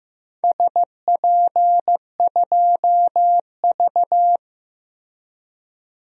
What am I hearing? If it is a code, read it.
Morse code, heard as a keyed tone, about 15 wpm, 700 Hz. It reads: SP2V
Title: DXCC-100 Call Signs - 4 Characters - Repeated 2x - 15wpm